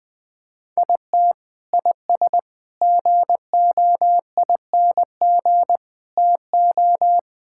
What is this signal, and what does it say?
Morse code, heard as a keyed tone, about 20 wpm, 700 Hz. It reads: IT IS GOING TO
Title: Sentences with 5-Word Phrases - Spoken Beginning - 20wpm